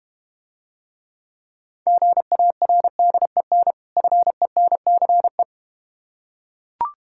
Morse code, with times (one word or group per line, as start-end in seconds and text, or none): 1.9-3.7 GARDEN
4.0-5.4 FENCE